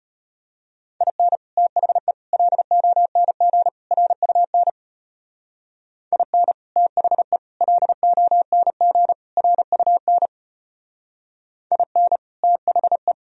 Morse code, heard as a keyed tone, about 31 wpm, 700 Hz. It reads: IN THE LONG RUN IN THE LONG RUN IN THE
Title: Speed-Racing - Sets of 4 Words 25wpm